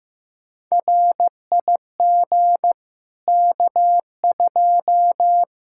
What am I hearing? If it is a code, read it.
Morse code, heard as a keyed tone, about 15 wpm, 700 Hz. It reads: RIG K2
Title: K1USN SST QSO Practice 15wpm